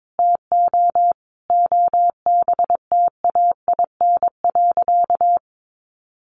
Morse code, heard as a keyed tone, about 22 wpm, 700 Hz. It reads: TO OBTAIN.
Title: Sentences from Top 1300 Words 22wpm